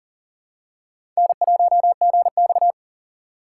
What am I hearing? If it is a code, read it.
Morse code, heard as a keyed tone, about 40 wpm, 700 Hz. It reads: N1GX
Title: USA Call Signs - 6x Character Spacing - 40wpm